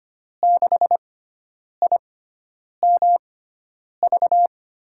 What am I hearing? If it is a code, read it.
Morse code, heard as a keyed tone, about 25 wpm, spaced as for 9 wpm, 700 Hz. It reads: 6IMV
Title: USA Call Signs - 6 Characters - 6x Character Spacing - 25wpm